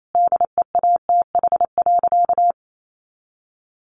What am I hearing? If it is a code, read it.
Morse code, heard as a keyed tone, about 28 wpm, 700 Hz. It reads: DEATH.
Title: Sentences from Top 1100 Words 28wpm